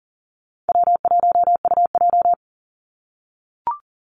Morse code, heard as a keyed tone, about 40 wpm, 700 Hz. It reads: W1UJ
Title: Easy Call Signs - 40wpm